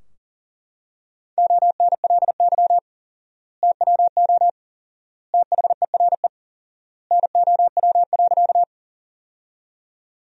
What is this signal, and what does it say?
Morse code, heard as a keyed tone, about 40 wpm, 700 Hz. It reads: ONLY TWO THERE NOW.
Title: 4x Word-Spacing - Sentences from Top 200 Words - 40wpm